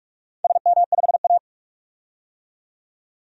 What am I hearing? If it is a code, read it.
Morse code, heard as a keyed tone, about 42 wpm, 700 Hz. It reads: SM5A
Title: DXCC-100 Call Signs - 4 Characters - 45wpm